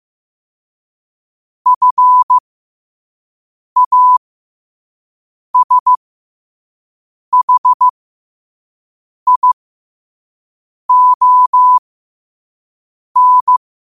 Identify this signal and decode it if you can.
Morse code, heard as a keyed tone, about 15 wpm, spaced as for 5 wpm, 1000 Hz. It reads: FASHION
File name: qotdfiles / 5wpm